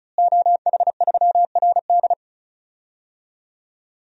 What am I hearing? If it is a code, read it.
Morse code, heard as a keyed tone, about 36 wpm, 700 Hz. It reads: OH3RD
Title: DXCC-100 Call Signs - 5 Characters - Repeated 3x - 35wpm